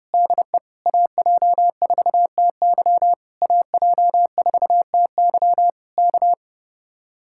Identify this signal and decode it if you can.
Morse code, heard as a keyed tone, about 33 wpm, 700 Hz. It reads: DE AJ4TY AJ4TY K
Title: Speed-Racing Common QSO Phrases - 30wpm